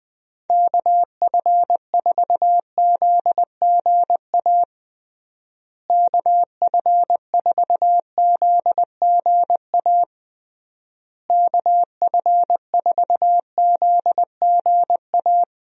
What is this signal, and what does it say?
Morse code, heard as a keyed tone, about 21 wpm, 700 Hz. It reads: KF4ZGA KF4ZGA KF4ZGA
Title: USA Call Signs - 6 Characters - Repeated 3x - 20wpm